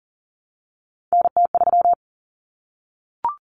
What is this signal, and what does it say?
Morse code, heard as a keyed tone, about 40 wpm, 700 Hz. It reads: NT3
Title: Common Cut Numbers - 3-Digit 40wpm